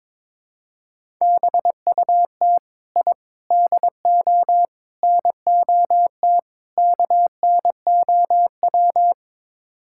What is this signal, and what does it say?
Morse code, heard as a keyed tone, about 22 wpm, 700 Hz. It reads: BUT I DO NOT KNOW
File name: practice / 22wpm / Build A Phrase - 5-Word Phrases - 22wpm